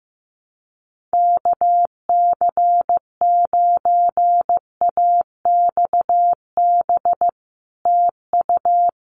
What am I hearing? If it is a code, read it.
Morse code, heard as a keyed tone, about 15 wpm, 700 Hz. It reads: KC9AXB TU
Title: Common QSO Phrases Encore 15wpm